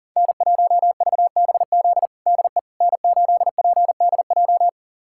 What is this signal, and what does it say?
Morse code, heard as a keyed tone, about 40 wpm, 700 Hz. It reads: N1VBZ DE N8PDJ
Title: USA Callsign DE Callsign - 5 Characters - 40wpm